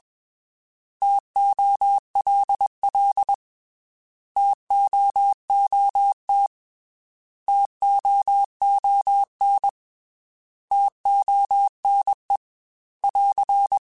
Morse code, heard as a keyed tone, about 21 wpm, 800 Hz. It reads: TOLL TOOT TOON TONE <AR>
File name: T Prefixes / T mp3s